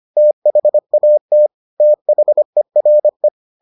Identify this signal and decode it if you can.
Morse code, heard as a keyed tone, about 25 wpm, 600 Hz. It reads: THAT THERE